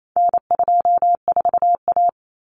Morse code, heard as a keyed tone, about 28 wpm, 700 Hz. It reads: N24A
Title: Common Cut Numbers - 4-Digit 28wpm